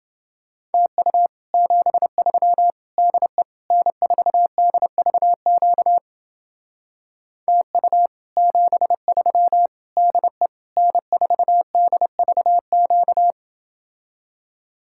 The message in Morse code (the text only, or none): TU 73 DE N4DVQ TU 73 DE N4DVQ